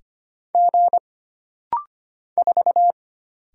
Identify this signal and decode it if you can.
Morse code, heard as a keyed tone, about 25 wpm, 700 Hz. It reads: G 4
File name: Single Character (Letter:Number:Punctuation:Prosign) - Rapid-Fire - 25wpm